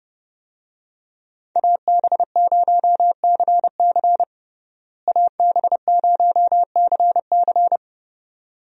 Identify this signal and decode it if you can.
Morse code, heard as a keyed tone, about 32 wpm, 700 Hz. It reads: AB0CC AB0CC
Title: USA Call Signs - 5 Characters - Repeated 2x - 30wpm